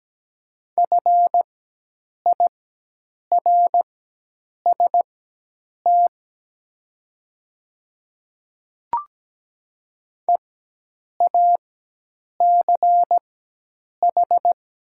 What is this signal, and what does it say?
Morse code, heard as a keyed tone, about 17 wpm, spaced as for 8 wpm, 700 Hz. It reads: FIRST EACH
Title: 4x Character Spacing - Top 100 Words - 17wpm